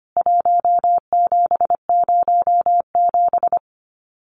Morse code, heard as a keyed tone, about 25 wpm, 700 Hz. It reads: 1707
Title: CWT Exchange 25wpm